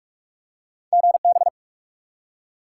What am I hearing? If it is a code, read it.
Morse code, heard as a keyed tone, about 46 wpm, 700 Hz. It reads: GB
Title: Top 90 QSO Elements 45wpm